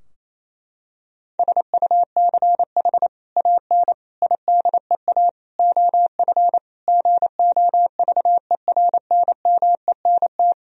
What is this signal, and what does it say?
Morse code, heard as a keyed tone, about 27 wpm, 700 Hz. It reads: SUCH AN IDEA OF GOVERNMENT
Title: Sentences from Top 1000 Words - Review Part 3 - 28wpm